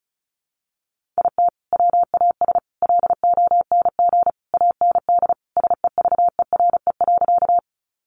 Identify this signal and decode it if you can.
Morse code, heard as a keyed tone, about 34 wpm, 700 Hz. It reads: IT WAS LONG AND SEVERE.
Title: Sentences from Top 1400 Words 35wpm